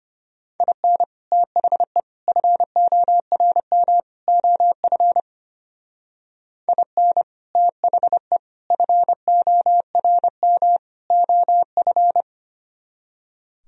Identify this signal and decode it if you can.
Morse code, heard as a keyed tone, about 31 wpm, 700 Hz. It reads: IN THE FORM OF IN THE FORM OF
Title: Speed-Racing - Sets of 4 Words 25wpm